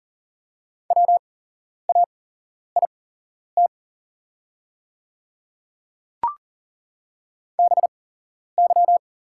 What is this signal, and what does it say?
Morse code, heard as a keyed tone, about 41 wpm, spaced as for 11 wpm, 700 Hz. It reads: WAIT BY
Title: 8x Character Spacing - Top 500 Words - 40wpm